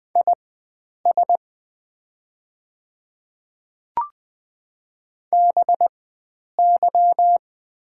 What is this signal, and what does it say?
Morse code, heard as a keyed tone, about 20 wpm, spaced as for 9 wpm, 690 Hz. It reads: IS BY
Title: 4x Character Spacing - Top 100 Words - 20wpm